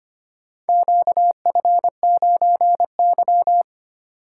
Morse code, heard as a keyed tone, about 25 wpm, 700 Hz. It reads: QF9Y
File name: practice / 25wpm / Four Letter-Number Combinations - 25wpm